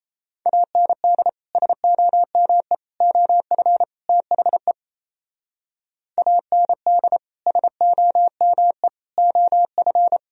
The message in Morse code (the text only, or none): AND SOME OF THE AND SOME OF